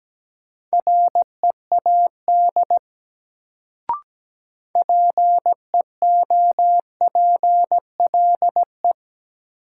Morse code, heard as a keyed tone, about 17 wpm, 700 Hz. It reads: READ PEOPLE
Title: Sentences from Top 200 Words - New Word Drill - 17wpm